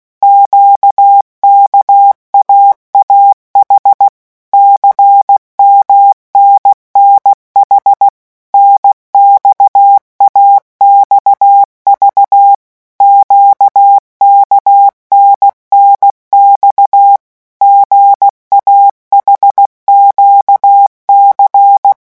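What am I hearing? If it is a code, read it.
Morse code, heard as a keyed tone, about 16 wpm, 800 Hz. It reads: QKAAH CMNNH NXAXV QKNNX GAHQC
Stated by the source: Morse letter-group revision drill at 15 words per minute.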